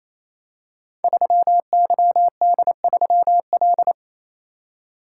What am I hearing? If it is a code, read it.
Morse code, heard as a keyed tone, about 29 wpm, 700 Hz. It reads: 3YD3L
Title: Code Groups - 5 Letter:Numbers - 28wpm